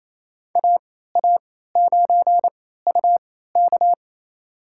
Morse code, heard as a keyed tone, about 28 wpm, spaced as for 16 wpm, 700 Hz. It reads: AA9UK
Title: USA Call Signs - 3x Character Spacing - 28wpm